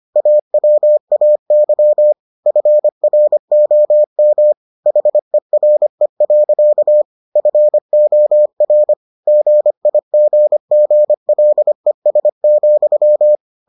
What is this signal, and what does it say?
Morse code, heard as a keyed tone, about 25 wpm, 600 Hz. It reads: AWAY FROM HERE. FOR GIGGLES,